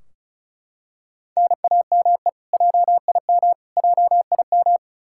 Morse code, heard as a keyed tone, about 36 wpm, 700 Hz. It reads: NAME JIM JIM
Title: Practice QSO - Exchange 1 - Beginning - 35wpm